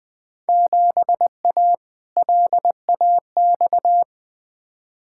Morse code, heard as a keyed tone, about 20 wpm, 700 Hz. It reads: 7A LAX
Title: Field Day Exchanges - 20wpm